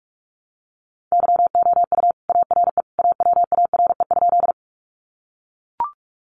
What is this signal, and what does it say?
Morse code, heard as a keyed tone, about 44 wpm, 700 Hz. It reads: YOU ARE AWARE?
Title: Sentences from Top 1200 Words 45wpm